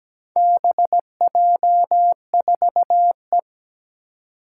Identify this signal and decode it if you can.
Morse code, heard as a keyed tone, about 17 wpm, 700 Hz. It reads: BJ4E
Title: Four Challenging Letter-Number Combinations 17wpm